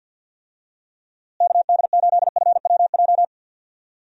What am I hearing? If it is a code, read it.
Morse code, heard as a keyed tone, about 51 wpm, 700 Hz. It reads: KD8FWJ
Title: USA Call Signs - 6 Characters - Repeated 3x - 50wpm